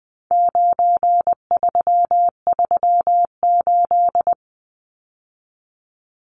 Morse code, heard as a keyed tone, about 20 wpm, 700 Hz. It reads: 9338
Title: Numbers - 4-Digit 20wpm